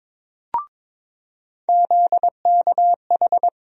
Morse code, heard as a keyed tone, about 22 wpm, 700 Hz. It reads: ZKH
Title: USA Call Sign Postfixes - 3 Letters 22wpm